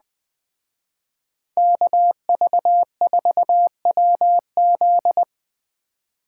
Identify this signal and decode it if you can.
Morse code, heard as a keyed tone, about 20 wpm, 700 Hz. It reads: KV4WZ
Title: Speed Racing - USA 2x2 Call Signs 17wpm